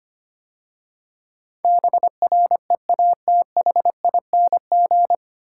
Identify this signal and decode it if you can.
Morse code, heard as a keyed tone, about 25 wpm, 700 Hz. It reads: BREATHING
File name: practice / 25wpm / 9 Letter Words - First 3 Letters Given - 25wpm